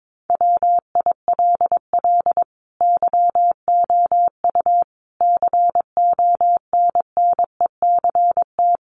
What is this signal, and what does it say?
Morse code, heard as a keyed tone, about 22 wpm, 700 Hz. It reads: WILL YOU CONNECT
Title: Sentences from Top 1500 Words 22wpm